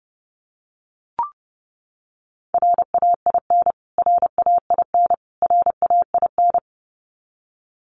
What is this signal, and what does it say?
Morse code, heard as a keyed tone, about 30 wpm, 700 Hz. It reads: RAIN RAIN RAIN
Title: Top 300 Words Repeated 30wpm